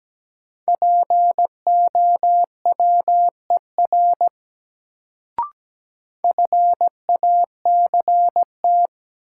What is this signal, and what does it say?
Morse code, heard as a keyed tone, about 17 wpm, 700 Hz. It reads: POWER FACT
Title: Sentences from Top 500 Words - New Word Drill - 17wpm